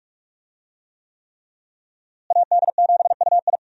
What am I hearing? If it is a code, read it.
Morse code, heard as a keyed tone, about 46 wpm, 700 Hz. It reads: AD7UI
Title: USA Call Signs - 5 Characters - Repeated 3x - 45wpm